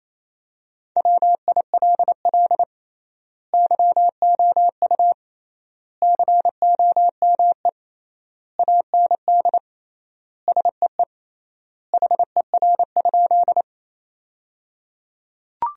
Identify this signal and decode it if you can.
Morse code, heard as a keyed tone, about 28 wpm, 700 Hz. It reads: WILL YOU COME AND SEE HER?
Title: 3x Word-Spacing - Sentences from Top 100 Words - 28wpm